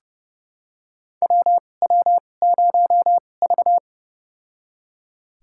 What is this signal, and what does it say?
Morse code, heard as a keyed tone, about 29 wpm, spaced as for 21 wpm, 700 Hz. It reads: WW0V
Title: USA Call Signs - 4 Characters - 2x Character Spacing - 30wpm